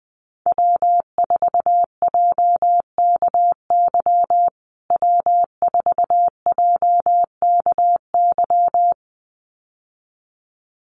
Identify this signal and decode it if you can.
Morse code, heard as a keyed tone, about 19 wpm, 700 Hz. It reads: W4JKY W4JKY
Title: USA Call Signs Repeated 2x 20wpm